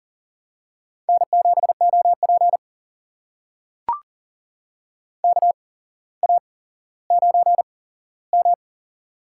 Morse code, heard as a keyed tone, about 40 wpm, spaced as for 17 wpm, 700 Hz. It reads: N7OP KA9M
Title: USA Call Signs - 4 Characters - 8x Character Spacing - 40wpm